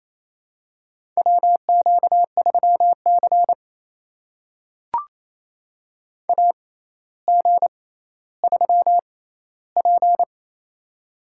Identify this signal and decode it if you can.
Morse code, heard as a keyed tone, about 28 wpm, spaced as for 14 wpm, 700 Hz. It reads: WQ3C AG3P
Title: USA Call Signs - 4 Characters - 6x Character Spacing - 28wpm